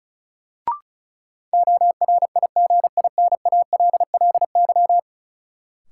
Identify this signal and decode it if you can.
Morse code, heard as a keyed tone, about 34 wpm, 700 Hz. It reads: ORIGINALLY
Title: Sentences from Top 1500 Words - New Word Drill - 35wpm